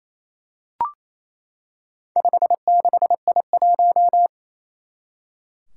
Morse code, heard as a keyed tone, about 27 wpm, 700 Hz. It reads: 56I1
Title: Four Challenging Letter-Number Combinations 28wpm